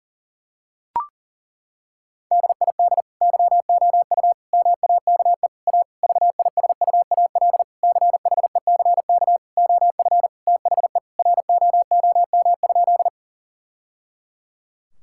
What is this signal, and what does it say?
Morse code, heard as a keyed tone, about 37 wpm, 700 Hz. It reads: DID YOU MAKE A VISUAL CHECK OF THE ROOM?
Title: Sentences from Top 1000 Words - Review Part 6 - 40wpm